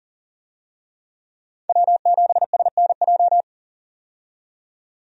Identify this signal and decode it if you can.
Morse code, heard as a keyed tone, about 38 wpm, 700 Hz. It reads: W7SNJ
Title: USA Call Signs - 5 Characters - 40wpm